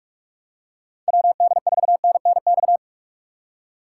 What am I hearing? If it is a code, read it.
Morse code, heard as a keyed tone, about 45 wpm, 700 Hz. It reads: WD4NNX
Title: USA Call Signs - 6 Characters - 3x Character Spacing - 45wpm